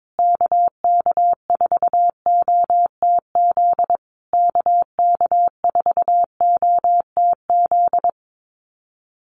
Morse code, heard as a keyed tone, about 22 wpm, 700 Hz. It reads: KK4OTZ KK4OTZ
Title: USA Call Signs Repeated 3x 22wpm